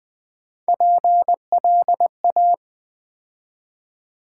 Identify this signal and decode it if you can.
Morse code, heard as a keyed tone, about 20 wpm, 700 Hz. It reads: PLA
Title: Build A Word - 3 Letters At A Time - 6-Letter Words - 20wpm